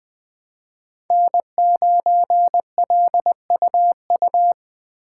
Morse code, heard as a keyed tone, about 20 wpm, 700 Hz. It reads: N9LUU
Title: USA Call Signs - 5 Characters - 3x Character Spacing - 20wpm